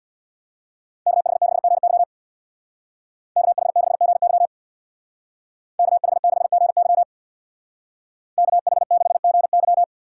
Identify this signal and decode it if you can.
Morse code, heard as a keyed tone, about 64 wpm, 700 Hz. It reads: KH6GY KH6GY KH6GY KH6GY